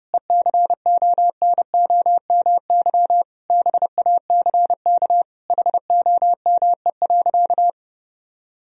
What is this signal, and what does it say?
Morse code, heard as a keyed tone, about 30 wpm, 700 Hz. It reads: ECONOMY BACK HOME.
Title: Sentences from Top 1000 Words - Review Part 3 - 30wpm